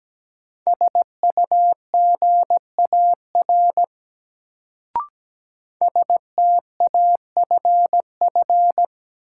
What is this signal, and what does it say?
Morse code, heard as a keyed tone, about 17 wpm, 700 Hz. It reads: SUGAR STAFF
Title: Sentences from Top 1200 Words - New Word Drill - 17wpm